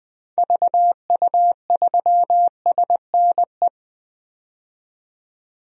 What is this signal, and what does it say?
Morse code, heard as a keyed tone, about 21 wpm, 700 Hz. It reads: VU3SNE
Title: DXCC-100 Call Signs - 6 Characters - 20wpm